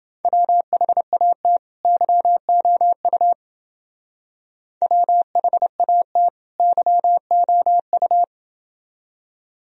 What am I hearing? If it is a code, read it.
Morse code, heard as a keyed tone, about 29 wpm, 700 Hz. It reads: WHAT YOU WHAT YOU